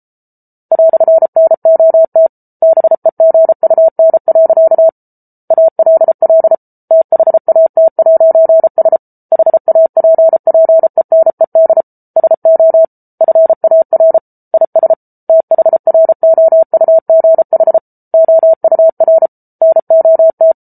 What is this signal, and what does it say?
Morse code, heard as a keyed tone, about 34 wpm, 650 Hz. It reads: "NOT BEGUN. ALL THAT'S HAPPENED SO FAR IS THROUGH OUR NOT